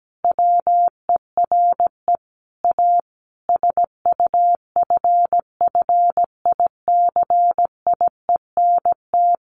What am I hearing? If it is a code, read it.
Morse code, heard as a keyed tone, about 17 wpm, 700 Hz. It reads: WERE A SUFFICIENT
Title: Sentences from Top 1400 Words 17wpm